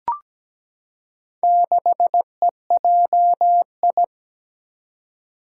Morse code, heard as a keyed tone, about 17 wpm, 700 Hz. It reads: 6EJI
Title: Four Challenging Letter-Number Combinations 17wpm